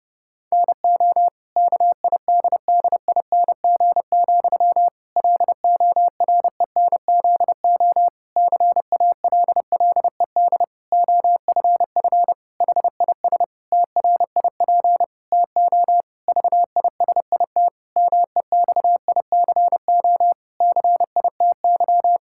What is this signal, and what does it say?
Morse code, heard as a keyed tone, about 32 wpm, 700 Hz. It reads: NO KIDDING, LORENZO CALLED OFF HIS TRIP TO VISIT MEXICO CITY
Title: Pangram Challenge v2 - 30wpm